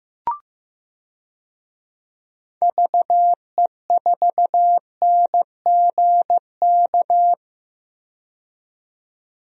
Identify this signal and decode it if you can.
Morse code, heard as a keyed tone, about 15 wpm, 700 Hz. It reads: VE4NGK
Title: DXCC-100 Call Signs - 6 Characters - 15wpm